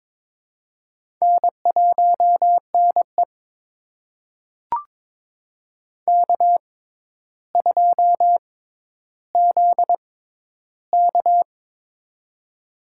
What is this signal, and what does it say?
Morse code, heard as a keyed tone, about 22 wpm, spaced as for 11 wpm, 700 Hz. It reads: N1NE K2ZK
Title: USA Call Signs - 4 Characters - 6x Character Spacing - 22wpm